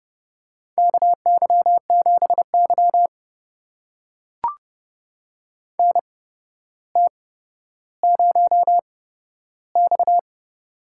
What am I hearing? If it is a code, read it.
Morse code, heard as a keyed tone, about 32 wpm, spaced as for 13 wpm, 700 Hz. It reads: KY7Y NT0X
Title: USA Call Signs - 8x Character Spacing - 30wpm